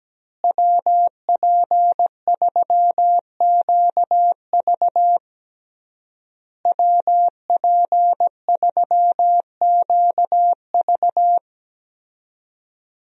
Morse code, heard as a keyed tone, about 17 wpm, 700 Hz. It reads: WP3QV WP3QV
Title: USA Call Signs - 5 Characters - Repeated 3x - 17wpm